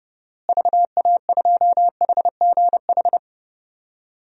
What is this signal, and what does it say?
Morse code, heard as a keyed tone, about 30 wpm, 700 Hz. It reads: VA2HGH
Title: DXCC-100 Call Signs - 6 Characters - Repeated 2x - 30wpm